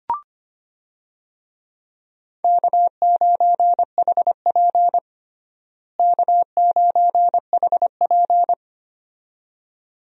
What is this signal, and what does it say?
Morse code, heard as a keyed tone, about 25 wpm, 700 Hz. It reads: K9HP K9HP
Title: USA Call Signs - 4 Characters - Repeated 2x - 25wpm